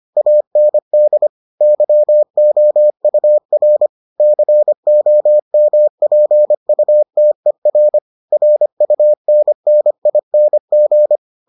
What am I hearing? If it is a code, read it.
Morse code, heard as a keyed tone, about 25 wpm, 600 Hz. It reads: AND YOUR COMPUTER RUNNING